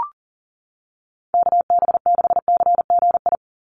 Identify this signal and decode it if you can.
Morse code, heard as a keyed tone, about 40 wpm, 700 Hz. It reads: KB6CGI
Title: USA Call Signs 40wpm